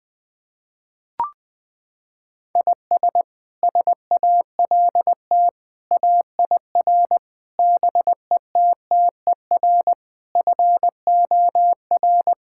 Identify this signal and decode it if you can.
Morse code, heard as a keyed tone, about 20 wpm, 700 Hz. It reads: IS SALT AIR BETTER FOR
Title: Sentences from Top 1000 Words - Review Part 5 - 20wpm